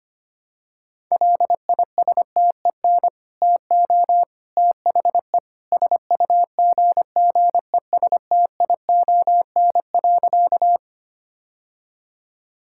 Morse code, heard as a keyed tone, about 25 wpm, 700 Hz. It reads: LISTEN TO THE SUGGESTION.
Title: Sentences from Top 1600 Words 25wpm